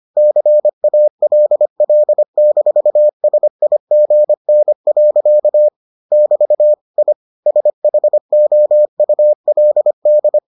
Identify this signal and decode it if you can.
Morse code, heard as a keyed tone, about 25 wpm, 600 Hz. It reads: CALL-SIGN. <BT> I SHOULD